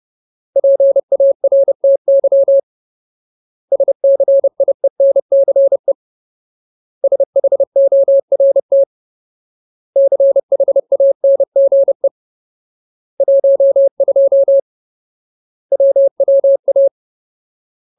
Morse code, heard as a keyed tone, about 30 wpm, 550 Hz. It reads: PARTY SCIENCE SHORT CHANGE 12 WWA
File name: Ultimate Word List / 30WPM Ultimate